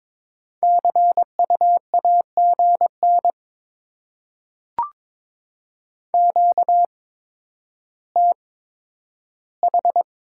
Morse code, heard as a keyed tone, about 22 wpm, spaced as for 11 wpm, 700 Hz. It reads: CU AGN QTH
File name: Top 20 QSO Elements - 8x Character Spacing - 22wpm